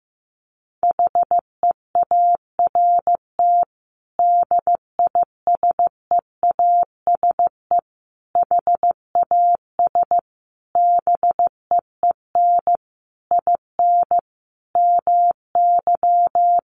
Morse code, heard as a keyed tone, about 15 wpm, 700 Hz. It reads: HEART DISEASE HAS BEEN IN MY
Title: Sentences from Top 1200 Words 15wpm